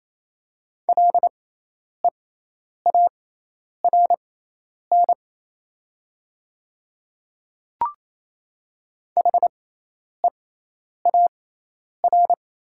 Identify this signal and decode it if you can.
Morse code, heard as a keyed tone, about 27 wpm, spaced as for 10 wpm, 700 Hz. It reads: LEARN HEAR
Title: 6x Character Spacing - Top 300 Words - 28wpm